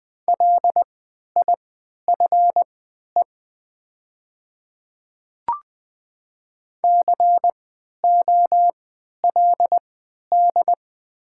Morse code, heard as a keyed tone, about 20 wpm, spaced as for 11 wpm, 700 Hz. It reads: LIFE COLD
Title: 3x Character Spacing - Top 500 Words - 20wpm